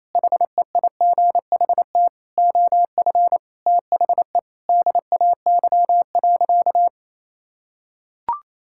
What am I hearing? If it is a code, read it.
Morse code, heard as a keyed tone, about 28 wpm, 700 Hz. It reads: HEIGHT OF THE DAY.
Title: Sentences from Top 1600 Words 28wpm